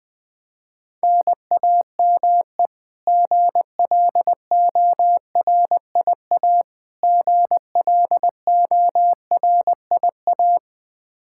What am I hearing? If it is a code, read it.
Morse code, heard as a keyed tone, about 20 wpm, 700 Hz. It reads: NAME GLORIA GLORIA
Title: K1USN SST QSO Practice 20wpm